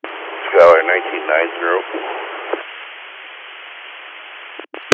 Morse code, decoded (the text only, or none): none